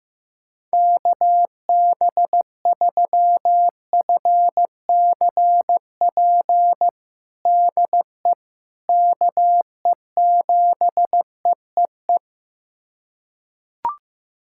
Morse code, heard as a keyed tone, about 15 wpm, 700 Hz. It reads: KB3FCP DE KE7EEE
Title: USA Callsign DE Callsign 15wpm